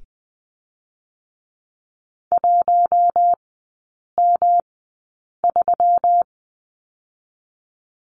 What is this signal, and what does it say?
Morse code, heard as a keyed tone, about 21 wpm, spaced as for 8 wpm, 700 Hz. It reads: 1M3
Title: 121 Lesson 20e - Number 3 - Three Characters 20wpm